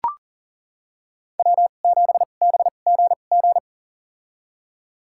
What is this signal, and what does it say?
Morse code, heard as a keyed tone, about 40 wpm, spaced as for 29 wpm, 690 Hz. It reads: W7BGG
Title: USA Call Signs - 2x Character Spacing - 40wpm